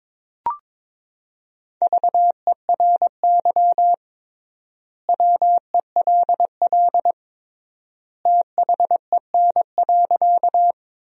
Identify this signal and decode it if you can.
Morse code, heard as a keyed tone, about 22 wpm, 700 Hz. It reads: VERY WELL THEN.
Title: 3x Word-Spacing - Sentences from Top 200 Words - 22wpm